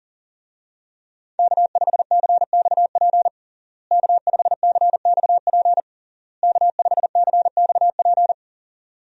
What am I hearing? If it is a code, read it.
Morse code, heard as a keyed tone, about 40 wpm, 700 Hz. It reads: K5CXP K5CXP K5CXP
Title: USA Call Signs - 5 Characters - Repeated 3x - 40wpm